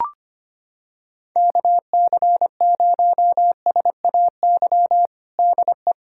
Morse code, heard as a keyed tone, about 25 wpm, 700 Hz. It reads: KC0SAY DE
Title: USA Callsign DE Callsign - 6 Characters - 25wpm